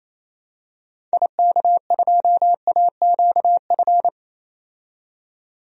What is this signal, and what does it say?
Morse code, heard as a keyed tone, about 28 wpm, 700 Hz. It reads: IK2AQF
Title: DXCC-100 Call Signs - 6 Characters - 28wpm